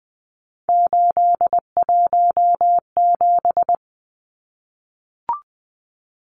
Morse code, heard as a keyed tone, about 20 wpm, 700 Hz. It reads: 817
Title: Numbers - 3-Digit 20wpm